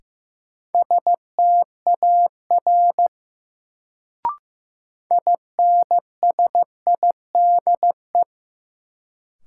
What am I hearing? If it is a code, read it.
Morse code, heard as a keyed tone, about 15 wpm, 700 Hz. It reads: STAR INSIDE
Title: Sentences from Top 1000 Words - Review Part 2 - New Word Drill - 15wpm